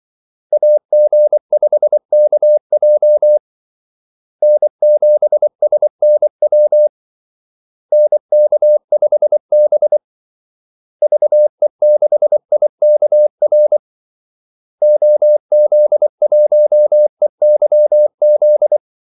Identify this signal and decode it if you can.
Morse code, heard as a keyed tone, about 24 wpm, 600 Hz. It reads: AG5KJ N7SNW NK5B VE6IKR OZ1EYZ
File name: Callsigns-24wpm